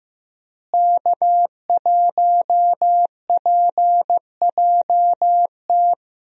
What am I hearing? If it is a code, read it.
Morse code, heard as a keyed tone, about 16 wpm, 700 Hz. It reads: K1PJT
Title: USA Call Signs - 5 Characters - Repeated 3x - 15wpm